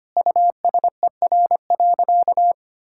Morse code, heard as a keyed tone, about 25 wpm, 700 Hz. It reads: USER.
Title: Sentences from Top 1000 Words - Review Part 5 - 25wpm